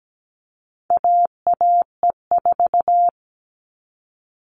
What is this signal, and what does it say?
Morse code, heard as a keyed tone, about 17 wpm, 700 Hz. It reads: AAE4
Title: All Cut Numbers - 4-Digit 17wpm